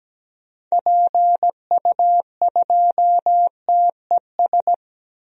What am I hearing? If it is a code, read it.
Morse code, heard as a keyed tone, about 17 wpm, 700 Hz. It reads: PU2TES
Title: DXCC-100 Call Signs - 6 Characters - 17wpm